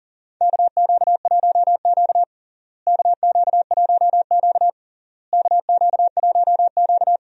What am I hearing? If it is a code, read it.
Morse code, heard as a keyed tone, about 40 wpm, 700 Hz. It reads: KQ1Q KQ1Q KQ1Q
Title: USA Call Signs - 4 Characters - Repeated 3x - 40wpm